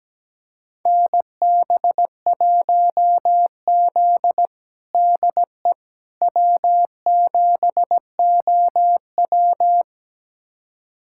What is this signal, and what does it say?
Morse code, heard as a keyed tone, about 17 wpm, 700 Hz. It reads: NB1Z DE W7OW
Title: USA Callsign DE Callsign - 4 Characters - 17wpm